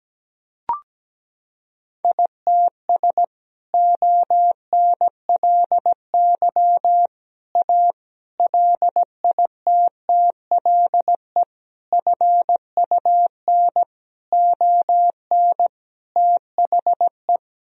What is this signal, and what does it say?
Morse code, heard as a keyed tone, about 17 wpm, 700 Hz. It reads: ITS ONLY A LITTLE FUN ON THE
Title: Sentences from Top 1000 Words - Review Part 4 - 17wpm